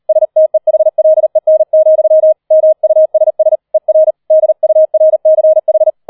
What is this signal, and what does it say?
Morse code, heard as a keyed tone, about 38 wpm, 630 Hz. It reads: STEHLEN, MUSS ER DURCH